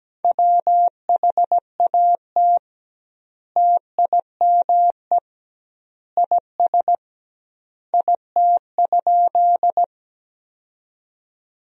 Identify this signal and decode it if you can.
Morse code, heard as a keyed tone, about 17 wpm, 700 Hz. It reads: WHAT TIME IS IT?
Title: 2x Word-Spacing - Sentences from Top 100 Words - 17wpm